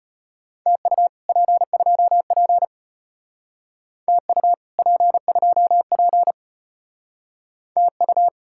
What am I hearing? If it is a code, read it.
Morse code, heard as a keyed tone, about 34 wpm, 700 Hz. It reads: TU P2P TU P2P TU